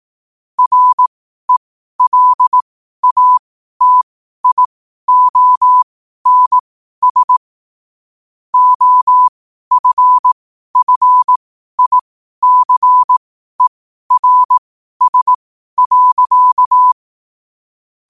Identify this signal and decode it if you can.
Morse code, heard as a keyed tone, about 18 wpm, spaced as for 12 wpm, 1000 Hz. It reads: RELATIONS OFFICERS.
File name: qotdfiles / 12wpm